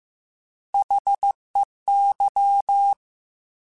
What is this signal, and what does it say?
Morse code, heard as a keyed tone, about 15 wpm, 800 Hz. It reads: HEY
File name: Alphabetical / Alphabetical mp3s / Hs @13